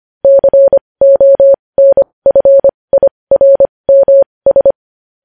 Morse code, heard as a keyed tone, about 25 wpm, spaced as for 20 wpm, 550 Hz. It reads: CONFIRMS
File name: podcast / News headlines-1x-25x20